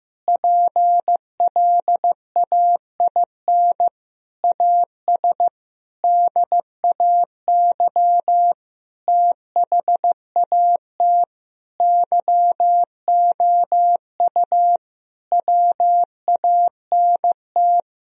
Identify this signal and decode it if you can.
Morse code, heard as a keyed tone, about 15 wpm, 700 Hz. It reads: PLAIN AS DAY THAT YOU WANT
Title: Sentences from Top 1000 Words - Review Part 2 - 15wpm